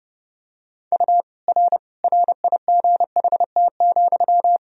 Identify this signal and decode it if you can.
Morse code, heard as a keyed tone, about 32 wpm, 700 Hz. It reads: U R RIGHT,
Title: Sentences from Top 400 Words and Abbreviations 30wpm